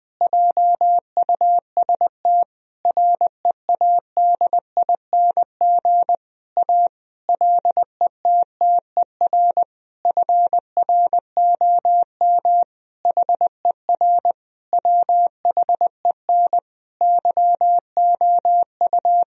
Morse code, heard as a keyed tone, about 20 wpm, 700 Hz. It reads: JUST READING A LETTER FROM HER WHEN YOU
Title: Sentences from Top 1000 Words - Review Part 3 - 20wpm